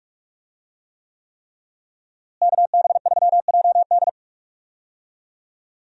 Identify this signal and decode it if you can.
Morse code, heard as a keyed tone, about 46 wpm, 700 Hz. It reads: KB3JD
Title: USA Call Signs - 5 Characters - Repeated 3x - 45wpm